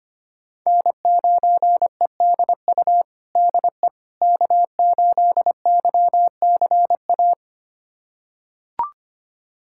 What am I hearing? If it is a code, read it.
Morse code, heard as a keyed tone, about 25 wpm, 700 Hz. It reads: N9EDU DE K8YCA
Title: USA Callsign DE Callsign - 5 Characters - 25wpm